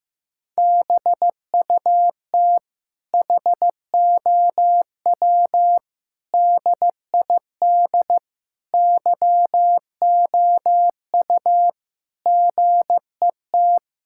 Morse code, heard as a keyed tone, about 15 wpm, 700 Hz. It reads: BUT HOW DID YOU GET
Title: Sentences from Top 1000 Words - Review Part 6 - 15wpm